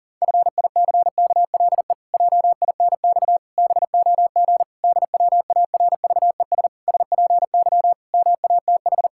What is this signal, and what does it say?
Morse code, heard as a keyed tone, about 40 wpm, 700 Hz. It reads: FICKLE JINX BOG DWARVES SPY MATH
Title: Pangram Challenge v2 - 40wpm